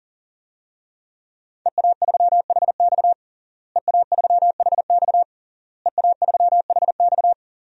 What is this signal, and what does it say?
Morse code, heard as a keyed tone, about 40 wpm, 700 Hz. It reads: EA3HX EA3HX EA3HX
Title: DXCC-100 Call Signs - 5 Characters - Repeated 3x - 40wpm